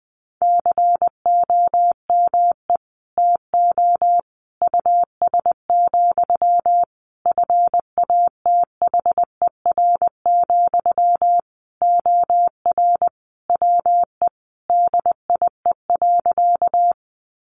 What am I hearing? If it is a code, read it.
Morse code, heard as a keyed tone, about 20 wpm, 700 Hz. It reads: COME TO US, FATHER, OR WE DIE.
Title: Sentences from Top 1300 Words 20wpm